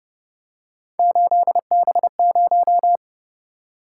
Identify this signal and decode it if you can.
Morse code, heard as a keyed tone, about 30 wpm, 700 Hz. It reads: 8B0